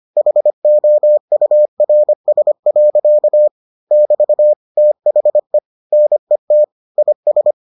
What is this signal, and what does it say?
Morse code, heard as a keyed tone, about 25 wpm, 600 Hz. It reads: HOURS. <BT> THE NET IS